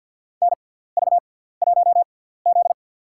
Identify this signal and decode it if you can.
Morse code, heard as a keyed tone, about 53 wpm, spaced as for 17 wpm, 700 Hz. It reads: NV1Z
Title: USA Call Signs - 4 Characters - 6x Character Spacing - 50wpm